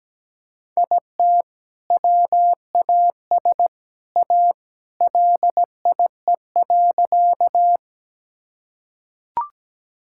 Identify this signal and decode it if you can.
Morse code, heard as a keyed tone, about 17 wpm, 700 Hz. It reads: IT WAS A LIE.
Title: Sentences from Top 1600 Words 17wpm